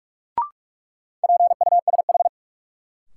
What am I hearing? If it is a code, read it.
Morse code, heard as a keyed tone, about 46 wpm, 700 Hz. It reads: PUSH
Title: Sentences from Top 1300 Words - New Word Drill - 45wpm